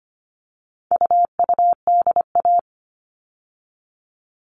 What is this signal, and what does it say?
Morse code, heard as a keyed tone, about 25 wpm, 700 Hz. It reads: UUDA
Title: All Cut Numbers - 4-Digit 25wpm